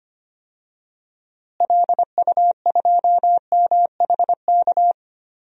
Morse code, heard as a keyed tone, about 25 wpm, 700 Hz. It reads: LU2MHK
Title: DXCC-100 Call Signs - 6 Characters - Repeated 3x - 25wpm